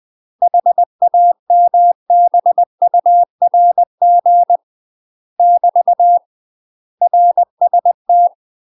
Morse code, heard as a keyed tone, about 20 wpm, 700 Hz. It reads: HAMBURG <BT> RST
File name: UK MASTER QSOs TRAINING/20WPM